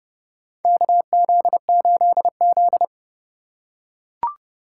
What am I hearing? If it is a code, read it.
Morse code, heard as a keyed tone, about 30 wpm, 700 Hz. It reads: KZ8Z
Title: USA Call Signs - 4 Characters - 2x Character Spacing - 30wpm